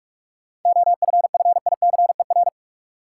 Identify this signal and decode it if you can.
Morse code, heard as a keyed tone, about 47 wpm, 700 Hz. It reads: OFFICER
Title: Seven Letter Words - 45wpm